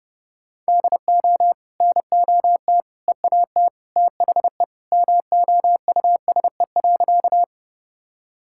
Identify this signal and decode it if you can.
Morse code, heard as a keyed tone, about 30 wpm, 700 Hz. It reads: DO NOT EAT THE MOUSE.
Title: Sentences from Top 1000 Words - Review Part 5 - 30wpm